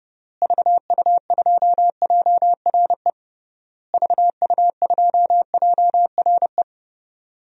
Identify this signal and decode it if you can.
Morse code, heard as a keyed tone, about 32 wpm, 700 Hz. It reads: VU2JRE VU2JRE
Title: DXCC-100 Call Signs - 6 Characters - Repeated 3x - 30wpm